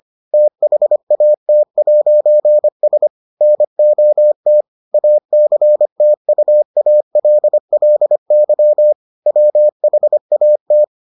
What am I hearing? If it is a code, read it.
Morse code, heard as a keyed tone, about 25 wpm, 600 Hz. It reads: THAT'S NOT ACTUALLY WHAT